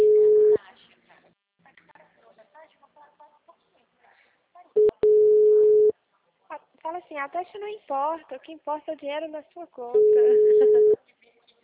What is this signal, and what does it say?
Ringback tone (Europe, 425 Hz), with speech.